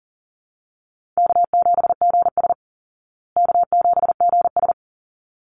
Morse code, heard as a keyed tone, about 40 wpm, 700 Hz. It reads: K7GS K7GS
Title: Easy Call Signs Repeated 2x - 40wpm